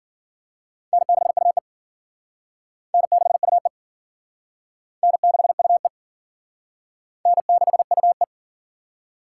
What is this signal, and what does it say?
Morse code, heard as a keyed tone, about 52 wpm, 700 Hz. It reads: N6UE N6UE N6UE N6UE